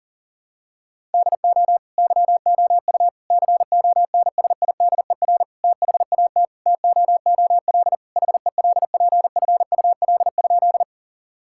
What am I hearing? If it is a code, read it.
Morse code, heard as a keyed tone, about 40 wpm, 700 Hz. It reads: DO YOU CONSIDER THAT TOOL HELPFUL?
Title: Sentences from Top 1000 Words - Review Part 6 - 40wpm